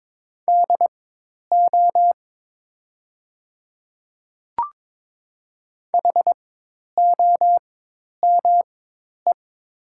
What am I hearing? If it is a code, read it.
Morse code, heard as a keyed tone, about 22 wpm, spaced as for 10 wpm, 700 Hz. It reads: DO HOME
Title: 4x Character Spacing - Top 200 Words - 22wpm